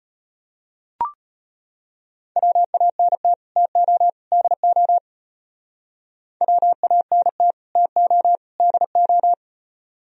Morse code, heard as a keyed tone, about 33 wpm, 700 Hz. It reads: WANT TO DO WANT TO DO